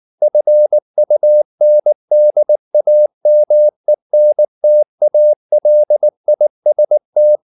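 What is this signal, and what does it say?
Morse code, heard as a keyed tone, about 19 wpm, 600 Hz. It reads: FUNDAMENTALIST
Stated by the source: Morse code at 19 wpm from CW med Gnister, reading Ken Follett's novel Never in English.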